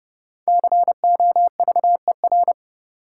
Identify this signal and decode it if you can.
Morse code, heard as a keyed tone, about 30 wpm, 700 Hz. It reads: COVER
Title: Sentences from Top 500 Words - New Word Drill - 30wpm